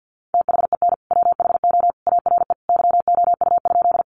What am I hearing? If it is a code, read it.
Morse code, heard as a keyed tone, about 50 wpm, 700 Hz. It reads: THEN WHO ARE YOU?
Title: Sentences from Top 100 Words 50wpm